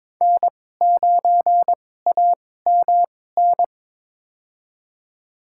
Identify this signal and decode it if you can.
Morse code, heard as a keyed tone, about 22 wpm, spaced as for 16 wpm, 700 Hz. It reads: N9AMN
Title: USA Call Signs - 5 Characters - 2x Character Spacing - 22wpm